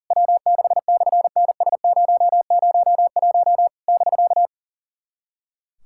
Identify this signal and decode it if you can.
Morse code, heard as a keyed tone, about 40 wpm, 700 Hz. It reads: W6/NS001 <BK>
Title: SOTA QSO Practice 40wpm